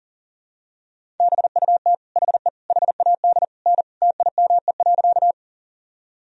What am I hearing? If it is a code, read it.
Morse code, heard as a keyed tone, about 40 wpm, 700 Hz. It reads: BUT HE HAD N TIME.
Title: Sentences from Top 100 Words and Abbreviations 40wpm